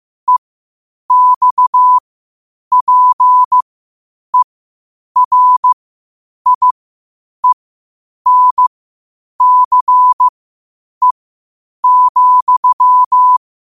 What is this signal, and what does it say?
Morse code, heard as a keyed tone, about 15 wpm, spaced as for 8 wpm, 1000 Hz. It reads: EXPERIENCE,
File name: qotdfiles / 8wpm